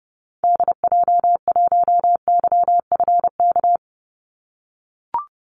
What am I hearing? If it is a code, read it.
Morse code, heard as a keyed tone, about 30 wpm, 700 Hz. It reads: DJ1YFK
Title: Call Signs - 30wpm